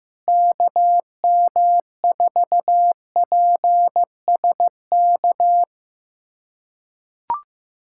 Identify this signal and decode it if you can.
Morse code, heard as a keyed tone, about 15 wpm, 700 Hz. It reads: KM4PSK
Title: USA Callsign DE Callsign 15wpm